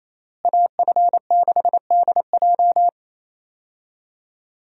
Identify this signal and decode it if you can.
Morse code, heard as a keyed tone, about 28 wpm, 700 Hz. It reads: AF6DJ
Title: USA Call Signs - 5 Characters - Repeated 3x - 28wpm